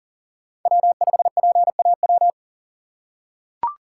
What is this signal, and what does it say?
Morse code, heard as a keyed tone, about 40 wpm, 700 Hz. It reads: W5PAW
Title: USA Call Signs - 5 Characters - 2x Character Spacing - 40wpm